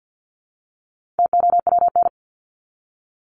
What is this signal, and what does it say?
Morse code, heard as a keyed tone, about 49 wpm, 700 Hz. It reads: TOWN
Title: Top 500 Words Repeated 50wpm